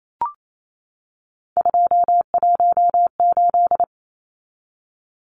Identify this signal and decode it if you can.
Morse code, heard as a keyed tone, about 28 wpm, 700 Hz. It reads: 218
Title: Numbers - 3-Digit 28wpm